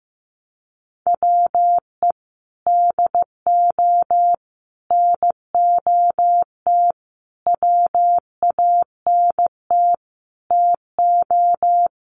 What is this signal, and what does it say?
Morse code, heard as a keyed tone, about 15 wpm, 700 Hz. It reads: WE DO NOT WANT TO
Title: Sets of 5 Words 15wpm